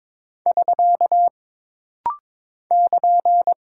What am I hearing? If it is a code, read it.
Morse code, heard as a keyed tone, about 22 wpm, 700 Hz. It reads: <SK> <KN>
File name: practice / 22wpm / Single Punctuation-Prosign - Rapid-Fire - 22wpm